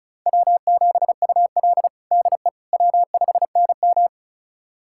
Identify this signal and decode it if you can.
Morse code, heard as a keyed tone, about 36 wpm, 700 Hz. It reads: W7UL DE W5NM
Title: USA Callsign DE Callsign - 4 Characters - 35wpm